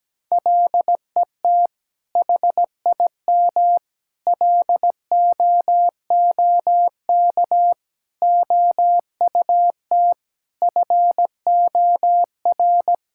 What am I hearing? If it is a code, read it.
Morse code, heard as a keyed tone, about 17 wpm, 700 Hz. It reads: LET HIM LOOK OUT FOR
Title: Sentences from Top 1600 Words 17wpm